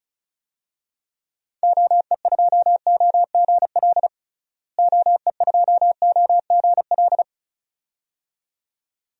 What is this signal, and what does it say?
Morse code, heard as a keyed tone, about 34 wpm, 700 Hz. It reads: OE2OGL OE2OGL
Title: DXCC-100 Call Signs - 6 Characters - Repeated 2x - 35wpm